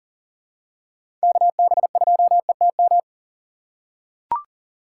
Morse code, heard as a keyed tone, about 38 wpm, 700 Hz. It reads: KB2ETM
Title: USA Call Signs - 6 Characters - 4x Character Spacing - 40wpm